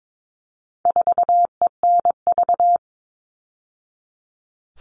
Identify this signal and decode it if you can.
Morse code, heard as a keyed tone, about 22 wpm, 700 Hz. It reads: 4ENV
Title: All Cut Numbers - 4-Digit 22wpm